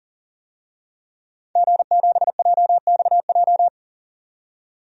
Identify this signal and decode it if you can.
Morse code, heard as a keyed tone, about 40 wpm, 700 Hz. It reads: G7JXJ
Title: DXCC-100 Call Signs - 5 Characters - 40wpm